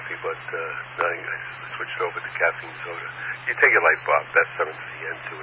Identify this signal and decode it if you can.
Emergency Alert System (SAME) tones, heard as harsh buzzing bursts, with speech. It is heard from 1.2 to 5.2 s.